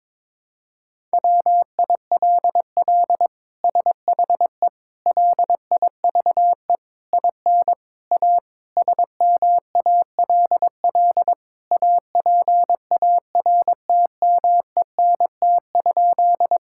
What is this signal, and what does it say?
Morse code, heard as a keyed tone, about 22 wpm, 700 Hz. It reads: WILL SHE LIVE IN A SMALL APARTMENT?
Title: Sentences from Top 1000 Words - Review Part 6 - 22wpm